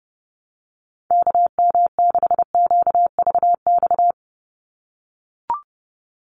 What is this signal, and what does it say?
Morse code, heard as a keyed tone, about 30 wpm, 700 Hz. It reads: KM6QVX
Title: USA Call Signs 30wpm